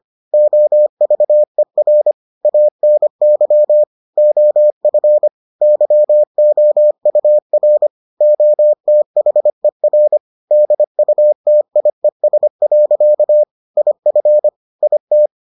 Morse code, heard as a keyed tone, about 25 wpm, 600 Hz. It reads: OVER ANY OF YOUR OTHER DUTIES. IF IT